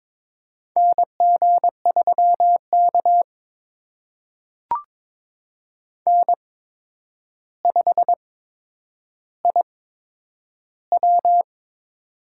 Morse code, heard as a keyed tone, about 22 wpm, spaced as for 9 wpm, 700 Hz. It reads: NG3K N5IW
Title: USA Call Signs - 4 Characters - 8x Character Spacing - 22wpm